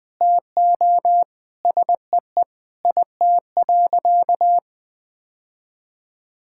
Morse code, heard as a keyed tone, about 20 wpm, 700 Hz. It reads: TO SEE IT.
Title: Sentences with 2-Word Phrases - 20wpm